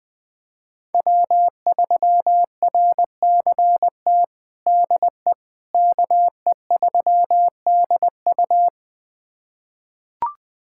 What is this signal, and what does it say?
Morse code, heard as a keyed tone, about 20 wpm, 700 Hz. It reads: W3RCT DE KE3DU
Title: USA Callsign DE Callsign - 5 Characters - 20wpm